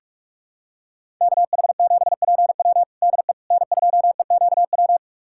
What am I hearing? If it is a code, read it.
Morse code, heard as a keyed tone, about 46 wpm, 700 Hz. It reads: KH7PW DE N2EQW
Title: USA Callsign DE Callsign - 5 Characters - 45wpm